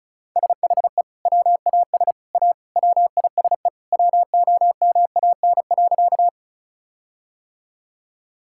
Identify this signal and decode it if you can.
Morse code, heard as a keyed tone, about 36 wpm, 700 Hz. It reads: SHE WAS A WISE WOMAN.
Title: Sentences from Top 1600 Words 35wpm